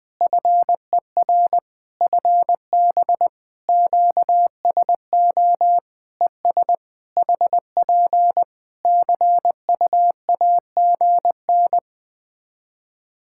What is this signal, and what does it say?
Morse code, heard as a keyed tone, about 20 wpm, 700 Hz. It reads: FER FB QSO ES HP CUAGN
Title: Practice QSO - Exchange 4 - Ending - 20wpm